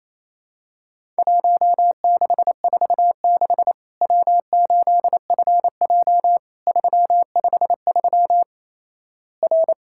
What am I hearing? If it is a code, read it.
Morse code, heard as a keyed tone, about 28 wpm, 700 Hz. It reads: 1646 W8FJ 353 R
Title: Worked All Europe DX Contest - QTC Exchange - 28wpm